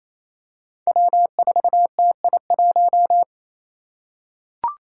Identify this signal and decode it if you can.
Morse code, heard as a keyed tone, about 28 wpm, 700 Hz. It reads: W4TI1
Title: Five Challenging Letter-Number Combinations 28wpm